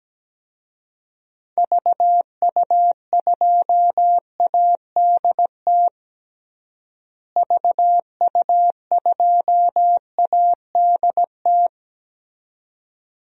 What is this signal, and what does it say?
Morse code, heard as a keyed tone, about 17 wpm, 700 Hz. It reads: VU2ADT VU2ADT
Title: DXCC-100 Call Signs - 6 Characters - Repeated 2x - 17wpm